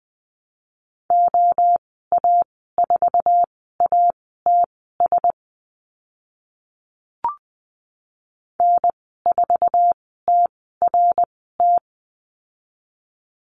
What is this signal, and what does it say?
Morse code, heard as a keyed tone, about 20 wpm, spaced as for 15 wpm, 700 Hz. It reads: OA4ATS N4TRT
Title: 062 Lesson 11g - Letter h - Call Signs 20wpm